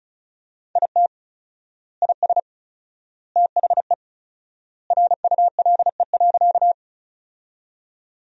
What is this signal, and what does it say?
Morse code, heard as a keyed tone, about 36 wpm, 700 Hz. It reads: IT IS THE RULE.
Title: 4x Word-Spacing - Sentences from Top 400 Words - 35wpm